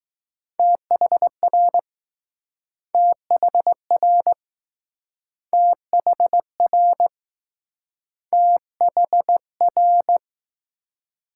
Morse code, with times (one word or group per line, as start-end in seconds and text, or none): 0.6-1.8 THR
2.9-4.3 THR
5.5-7.1 THR
8.3-10.2 THR